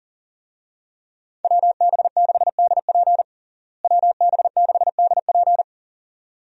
Morse code, heard as a keyed tone, about 40 wpm, 700 Hz. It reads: WB6DP WB6DP
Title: USA Call Signs - 5 Characters - Repeated 2x - 40wpm